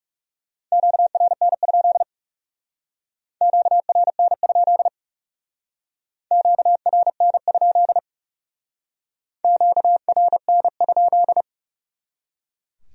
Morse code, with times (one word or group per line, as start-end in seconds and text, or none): 0.7-2.0 QRN?
3.4-4.9 QRN?
6.3-8.0 QRN?
9.4-11.4 QRN?